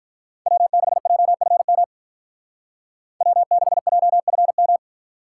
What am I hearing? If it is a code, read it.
Morse code, heard as a keyed tone, about 44 wpm, 700 Hz. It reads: W6JFM W6JFM
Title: Speed Racing - USA 1x3 Call Signs 35wpm